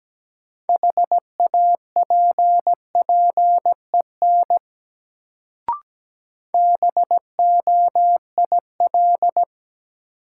Morse code, heard as a keyed tone, about 17 wpm, 700 Hz. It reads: HAPPEN BOIL
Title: Sets of 5 Words Encore - New Word Drill - 17wpm